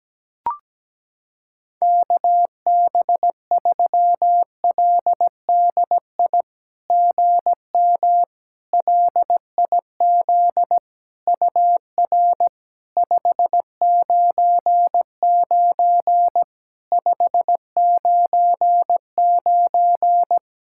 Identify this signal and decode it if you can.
Morse code, heard as a keyed tone, about 17 wpm, 700 Hz. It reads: KB3LDI GM LIZ UR 599 599
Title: SOTA QSO Practice 17wpm